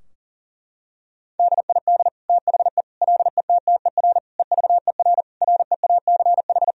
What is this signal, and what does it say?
Morse code, heard as a keyed tone, about 40 wpm, 700 Hz. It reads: DID THE LETTER EVER REACH
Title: Sentences from Top 1000 Words - Review Part 2 - 40wpm